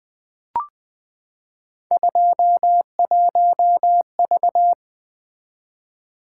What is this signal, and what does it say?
Morse code, heard as a keyed tone, about 20 wpm, 700 Hz. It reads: 21V
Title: Three Letter-Number Combinations - 20wpm